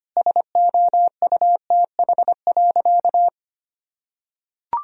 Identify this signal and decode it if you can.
Morse code, heard as a keyed tone, about 25 wpm, 700 Hz. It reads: SOUTH.
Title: Sentences from Top 1000 Words - Review Part 2 - 25wpm